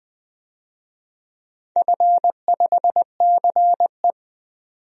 Morse code, heard as a keyed tone, about 20 wpm, 700 Hz. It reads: F5CE
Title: DXCC-100 Call Signs - 4 Characters - 20wpm